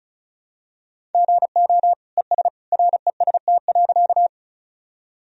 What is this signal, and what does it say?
Morse code, heard as a keyed tone, about 36 wpm, 700 Hz. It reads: GO ES REST.
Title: Sentences from Top 500 Words and Abbreviations 35wpm